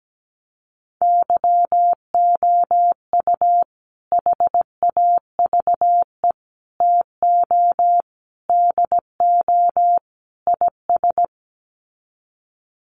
Morse code, heard as a keyed tone, about 17 wpm, 700 Hz. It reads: YOU HAVE TO DO IS
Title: Sets of 5 Words 17wpm